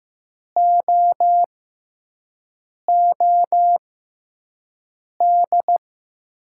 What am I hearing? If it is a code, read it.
Morse code, heard as a keyed tone, about 15 wpm, spaced as for 5 wpm, 700 Hz. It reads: OOD
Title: 6x Character Spacing - Top 300 Words - 15wpm